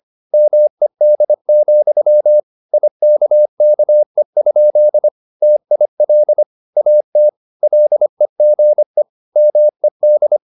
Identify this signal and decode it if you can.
Morse code, heard as a keyed tone, about 25 wpm, 600 Hz. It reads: MED, IKKE? TIL AT LEGE MED
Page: Ferie 25 wpm